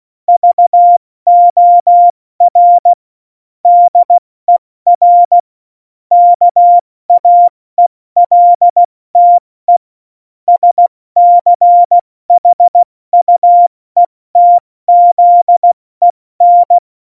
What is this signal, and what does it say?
Morse code, heard as a keyed tone, about 16 wpm, spaced as for 14 wpm, 700 Hz. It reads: VOR DER KAELTE SCHUETZEN
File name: Morsekurs / Witze / Witz 21. Januar 2020, 70 F80